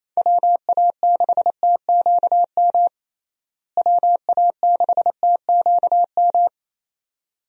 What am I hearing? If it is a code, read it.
Morse code, heard as a keyed tone, about 28 wpm, 700 Hz. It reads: WA6TQM WA6TQM
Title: USA Call Signs - 6 Characters - Repeated 2x - 28wpm